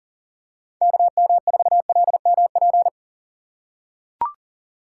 Morse code, heard as a keyed tone, about 40 wpm, 700 Hz. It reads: KM4LMP
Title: USA Call Signs - 6 Characters - 2x Character Spacing - 40wpm